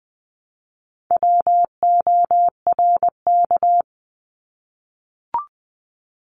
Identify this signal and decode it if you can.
Morse code, heard as a keyed tone, about 20 wpm, 700 Hz. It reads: WORK
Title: Multi-letter Phonemes 20wpm